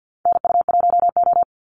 Morse code, heard as a keyed tone, about 50 wpm, 700 Hz. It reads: NU1O
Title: Call Signs Repeated 2x - 50wpm